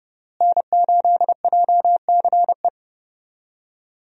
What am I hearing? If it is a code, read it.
Morse code, heard as a keyed tone, about 32 wpm, 700 Hz. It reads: N8JCE
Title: USA Call Signs - 5 Characters - Repeated 3x - 30wpm